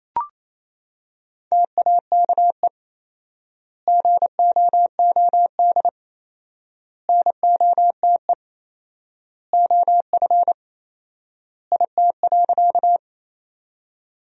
Morse code, heard as a keyed tone, about 28 wpm, 700 Hz. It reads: TAKE GOOD NOTE OF IT.
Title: 4x Word-Spacing - Sentences from Top 500 Words - 28wpm